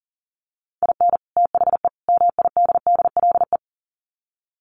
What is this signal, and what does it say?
Morse code, heard as a keyed tone, about 40 wpm, 700 Hz. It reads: IN THE MIDDLE
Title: Sets of 3 Words 40wpm